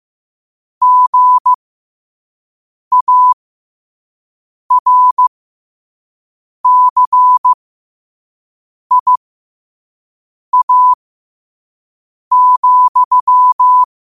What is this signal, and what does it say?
Morse code, heard as a keyed tone, about 15 wpm, spaced as for 5 wpm, 1000 Hz. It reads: GARCIA,
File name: qotdfiles / 5wpm